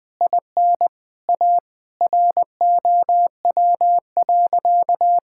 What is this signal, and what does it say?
Morse code, heard as a keyed tone, about 20 wpm, 700 Hz. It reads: IN A ROW.
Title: Sentences with 7-Word Phrases - Spoken Beginning - 20wpm